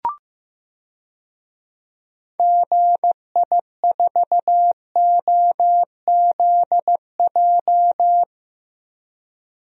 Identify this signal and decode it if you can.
Morse code, heard as a keyed tone, about 15 wpm, 700 Hz. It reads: GI4OZJ
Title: DXCC-100 Call Signs - 6 Characters - 15wpm